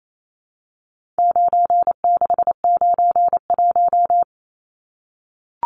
Morse code, heard as a keyed tone, about 28 wpm, 700 Hz. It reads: 9691
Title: Numbers - 4-Digit 28wpm